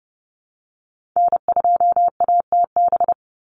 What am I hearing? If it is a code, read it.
Morse code, heard as a keyed tone, about 30 wpm, 700 Hz. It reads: N2ATB
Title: Call Signs Repeated 3x - 30wpm